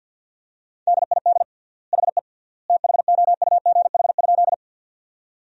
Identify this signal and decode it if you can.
Morse code, heard as a keyed tone, about 50 wpm, 700 Hz. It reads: DID HE THOUGH?
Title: 3x Word-Spacing - Sentences from Top 400 Words - 50wpm